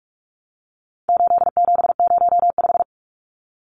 Morse code, heard as a keyed tone, about 44 wpm, 700 Hz. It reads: 8705
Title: Numbers - 4-Digit 45wpm